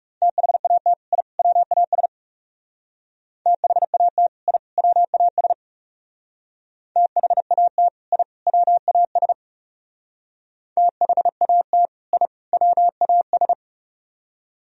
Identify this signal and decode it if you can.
Morse code, heard as a keyed tone, about 39 wpm, 700 Hz. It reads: THAT I WAS THAT I WAS THAT I WAS THAT I WAS